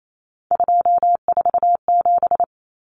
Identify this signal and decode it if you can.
Morse code, heard as a keyed tone, about 28 wpm, 700 Hz. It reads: 247
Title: Numbers - 3-Digit 28wpm